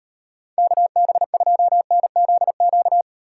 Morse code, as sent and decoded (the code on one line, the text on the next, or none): -.- -... ..--- -. --.. --.-
KB2NZQ